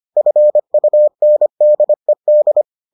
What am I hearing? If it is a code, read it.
Morse code, heard as a keyed tone, about 25 wpm, 600 Hz. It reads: FUNDED